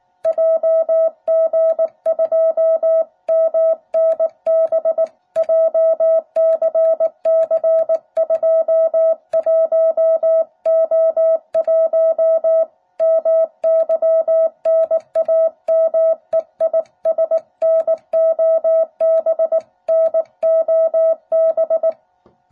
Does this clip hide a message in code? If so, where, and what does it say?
Yes: Morse code from 0.3 to 21.9 s, heard as a keyed tone, about 19 wpm, 660 Hz. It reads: JG2MNB JCC21O1 MY NAME IS NOB NOB